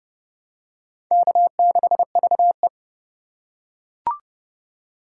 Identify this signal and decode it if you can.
Morse code, heard as a keyed tone, about 29 wpm, 700 Hz. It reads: K6VE